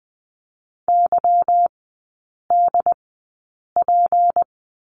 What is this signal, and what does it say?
Morse code, heard as a keyed tone, about 20 wpm, spaced as for 8 wpm, 700 Hz. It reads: YDP
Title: 152 Lesson 25e - Letter p - Three Characters 20wpm